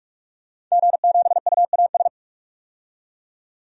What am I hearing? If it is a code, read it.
Morse code, heard as a keyed tone, about 44 wpm, 700 Hz. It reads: G7UAS
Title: DXCC-100 Call Signs - 5 Characters - 45wpm